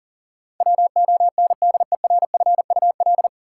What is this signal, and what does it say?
Morse code, heard as a keyed tone, about 38 wpm, 700 Hz. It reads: WONDERFUL
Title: Sentences from Top 1100 Words - New Word Drill Alt - 40wpm